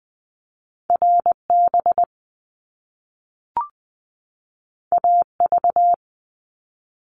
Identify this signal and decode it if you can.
Morse code, heard as a keyed tone, about 20 wpm, 700 Hz. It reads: RB AV
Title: Top Two Letter Combinations in Words 20wpm